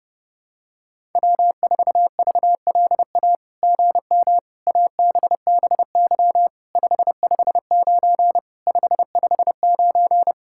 Morse code, heard as a keyed tone, about 31 wpm, 700 Hz. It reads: W4VLA GM ABBY 559 559
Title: SOTA QSO Practice 30wpm